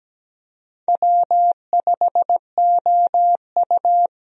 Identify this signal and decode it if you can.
Morse code, heard as a keyed tone, about 17 wpm, 700 Hz. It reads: W5OU
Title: USA Call Signs - 4 Characters - 17wpm